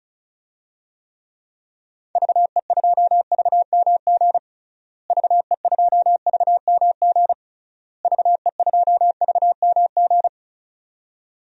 Morse code, heard as a keyed tone, about 34 wpm, 700 Hz. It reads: VE2VMG VE2VMG VE2VMG
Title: DXCC-100 Call Signs - 6 Characters - Repeated 3x - 35wpm